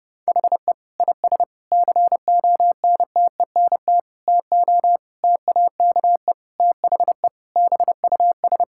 Morse code, heard as a keyed tone, about 30 wpm, 700 Hz. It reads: HE IS CONTENT TO TAKE THE BUS
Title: Sentences from Top 1000 Words - Review Part 6 - 30wpm